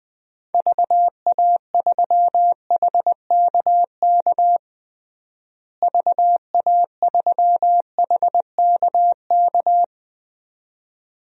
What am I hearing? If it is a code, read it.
Morse code, heard as a keyed tone, about 20 wpm, 700 Hz. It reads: VA3HKK VA3HKK
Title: DXCC-100 Call Signs - 6 Characters - Repeated 2x - 20wpm